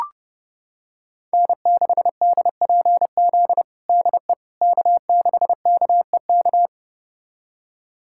Morse code, heard as a keyed tone, about 32 wpm, 700 Hz. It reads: N6DPZ DE K6KEK
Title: USA Callsign DE Callsign - 5 Characters - 30wpm